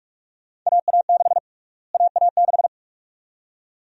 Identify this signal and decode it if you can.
Morse code, heard as a keyed tone, about 46 wpm, 700 Hz. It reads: AA6 AA6
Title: USA Call Sign Prefixes - Repeated 2x - 45wpm